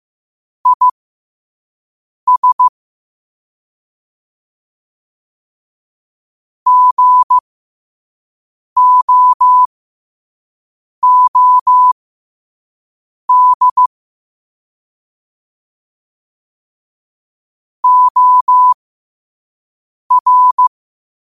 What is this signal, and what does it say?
Morse code, heard as a keyed tone, about 15 wpm, spaced as for 5 wpm, 1000 Hz. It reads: IS GOOD OR
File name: qotdfiles / 5wpm